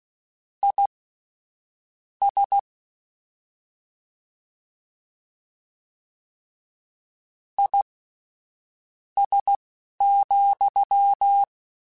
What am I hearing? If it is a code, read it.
Morse code, heard as a keyed tone, about 16 wpm, spaced as for 7 wpm, 800 Hz. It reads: IS IS,
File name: General / General-mp3s / Suffixes sp @10